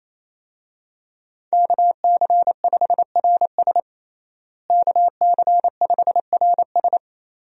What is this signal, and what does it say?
Morse code, heard as a keyed tone, about 27 wpm, 700 Hz. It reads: KC5RS KC5RS
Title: USA Call Signs - 5 Characters - Repeated 2x - 28wpm